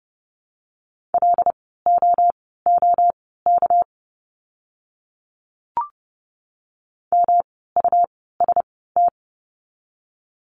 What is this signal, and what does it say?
Morse code, heard as a keyed tone, about 30 wpm, spaced as for 17 wpm, 700 Hz. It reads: LOOK MUST
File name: course / 30:10wpm / 237 Lesson 38f - Letter q - Words 30wpm